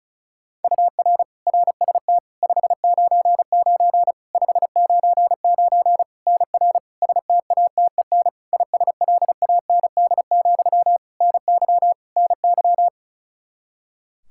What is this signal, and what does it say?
Morse code, heard as a keyed tone, about 34 wpm, 700 Hz. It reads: UR RST 599 599 NR STATEN ISLAND, NY NY
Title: Practice QSO - Exchange 1 - Beginning - 35wpm